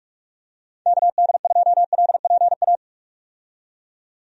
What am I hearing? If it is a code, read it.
Morse code, heard as a keyed tone, about 44 wpm, 700 Hz. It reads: KD2LPA
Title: USA Call Signs - 6 Characters - 45wpm